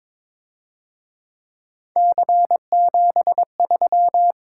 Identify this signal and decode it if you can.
Morse code, heard as a keyed tone, about 26 wpm, 700 Hz. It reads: C73
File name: Speed-Racing - Three Letter-Number Combinations - 22wpm